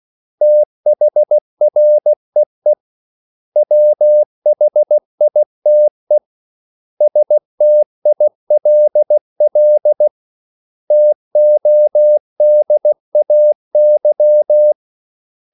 Morse code, heard as a keyed tone, about 16 wpm, 600 Hz. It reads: THREE WHITE STILL TODAY
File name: CW / Practice / 500MostCommonEnglishWords16